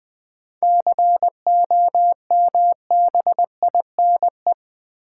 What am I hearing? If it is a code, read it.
Morse code, heard as a keyed tone, about 21 wpm, 700 Hz. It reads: COMBINE
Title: Sentences from Top 1600 Words - New Word Drill - 20wpm